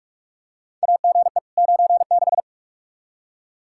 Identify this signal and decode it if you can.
Morse code, heard as a keyed tone, about 42 wpm, 700 Hz. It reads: AGE 96
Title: Practice QSO - Exchange 3 - More Info - 45wpm